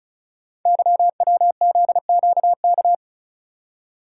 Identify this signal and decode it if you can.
Morse code, heard as a keyed tone, about 34 wpm, 690 Hz. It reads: YWZQK
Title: Code Groups - 5 Letters - 35wpm